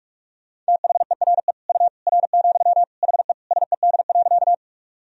Morse code, heard as a keyed tone, about 46 wpm, 700 Hz. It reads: THERE U R, HE SED.
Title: Sentences from Top 100 Words and Abbreviations 45wpm